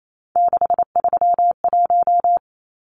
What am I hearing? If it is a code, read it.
Morse code, heard as a keyed tone, about 28 wpm, 700 Hz. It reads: 631
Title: Numbers - 3-Digit 28wpm